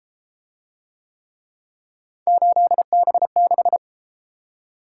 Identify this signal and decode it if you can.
Morse code, heard as a keyed tone, about 33 wpm, 700 Hz. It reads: 8B6